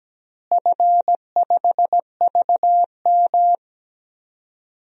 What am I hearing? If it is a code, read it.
Morse code, heard as a keyed tone, about 17 wpm, 700 Hz. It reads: F5VM
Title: DXCC-100 Call Signs - 4 Characters - Repeated 3x - 17wpm